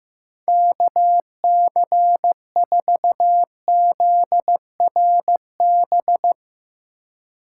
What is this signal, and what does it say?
Morse code, heard as a keyed tone, about 15 wpm, 700 Hz. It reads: KC4ZRB
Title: Speed Racing - USA 2x3 Call Signs 15wpm